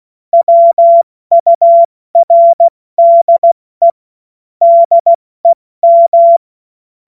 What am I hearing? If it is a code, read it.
Morse code, heard as a keyed tone, about 16 wpm, 680 Hz. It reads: WURDE DEM
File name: Morsekurs / Witze / Witz 21.November 2023 70 F80